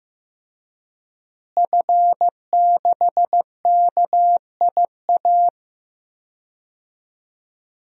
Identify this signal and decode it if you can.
Morse code, heard as a keyed tone, about 15 wpm, 700 Hz. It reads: F6KIA
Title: DXCC-100 Call Signs - 5 Characters - Repeated 3x - 15wpm